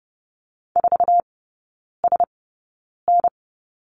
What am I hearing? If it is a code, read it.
Morse code, heard as a keyed tone, about 30 wpm, spaced as for 9 wpm, 700 Hz. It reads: 4SN
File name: Lesson 16e - Letter c - Three Characters 30wpm